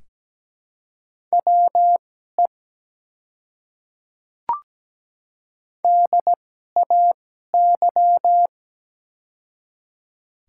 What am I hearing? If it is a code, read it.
Morse code, heard as a keyed tone, about 17 wpm, spaced as for 12 wpm, 690 Hz. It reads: WE DAY
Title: 2x Character Spacing - Top 100 Words - 17wpm